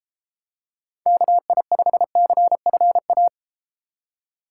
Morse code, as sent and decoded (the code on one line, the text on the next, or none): -.- .. ..... -.-. ..-. .-
KI5CFA